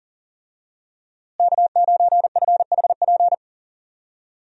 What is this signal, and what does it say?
Morse code, heard as a keyed tone, about 37 wpm, 700 Hz. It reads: K9FHP
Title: USA Call Signs - 5 Characters - Repeated 3x - 40wpm